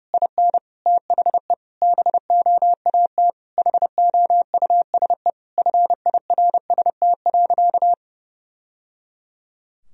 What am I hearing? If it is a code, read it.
Morse code, heard as a keyed tone, about 30 wpm, 700 Hz. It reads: IN THE BOAT HOUSE FIRST.
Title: Sentences from Top 1000 Words - Review Part 3 - 30wpm